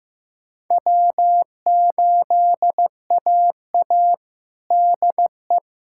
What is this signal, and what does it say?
Morse code, heard as a keyed tone, about 16 wpm, 700 Hz. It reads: W8AA DE
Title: USA Callsign DE Callsign - 4 Characters - 15wpm